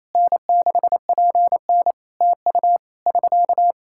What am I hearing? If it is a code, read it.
Morse code, heard as a keyed tone, about 28 wpm, 700 Hz. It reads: N6PN TU <SK>
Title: K1USN SST QSO Practice 28wpm